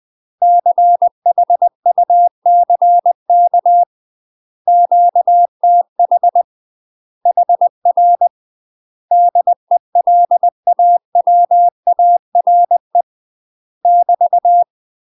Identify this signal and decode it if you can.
Morse code, heard as a keyed tone, about 20 wpm, 700 Hz. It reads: CHUCK QTH HR DELAWARE <BT>
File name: UK MASTER QSOs TRAINING / 20WPM / UK MASTER QSOs 0013